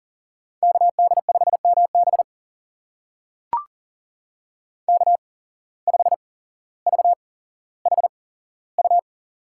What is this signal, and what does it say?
Morse code, heard as a keyed tone, about 40 wpm, spaced as for 17 wpm, 700 Hz. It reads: KD5MB K5VHU
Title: USA Call Signs - 5 Characters - 8x Character Spacing - 40wpm